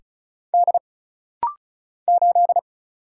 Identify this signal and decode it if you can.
Morse code, heard as a keyed tone, about 36 wpm, 700 Hz. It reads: D 8
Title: Single Letter-Number Alt - Rapid-Fire 35wpm